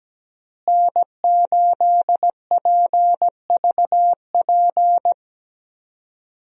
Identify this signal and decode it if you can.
Morse code, heard as a keyed tone, about 17 wpm, 700 Hz. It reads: N8PVP
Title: USA Call Signs - 5 Characters - Repeated 3x - 17wpm